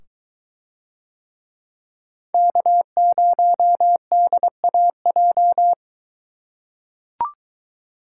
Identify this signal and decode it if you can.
Morse code, heard as a keyed tone, about 23 wpm, 700 Hz. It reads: K0DAJ